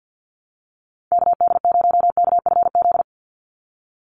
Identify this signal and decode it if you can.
Morse code, heard as a keyed tone, about 50 wpm, 700 Hz. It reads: KD0KFZ
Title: USA Call Signs 50wpm